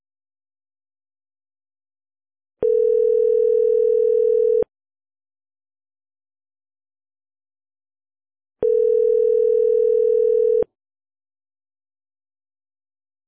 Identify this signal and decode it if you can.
Ringback tone (North America), 440 + 480 Hz.